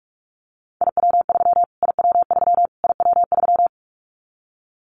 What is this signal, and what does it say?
Morse code, heard as a keyed tone, about 44 wpm, 700 Hz. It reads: IW3 IW3 IW3
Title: Callsign Prefixes for Top 25 DXCC - Repeated 3x - 45wpm